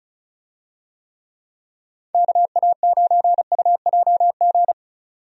Morse code, heard as a keyed tone, about 34 wpm, 700 Hz. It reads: KA9UJG
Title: USA Call Signs - 6 Characters - 35wpm